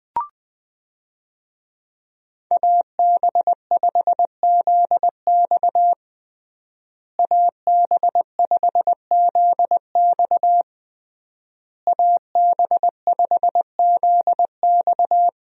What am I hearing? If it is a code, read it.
Morse code, heard as a keyed tone, about 21 wpm, 700 Hz. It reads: AB5ZX AB5ZX AB5ZX
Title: USA Call Signs - 5 Characters - Repeated 3x - 20wpm